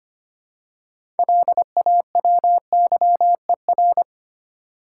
Morse code, heard as a keyed tone, about 25 wpm, 700 Hz. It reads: LAWYER
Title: Six Letter Words - 25wpm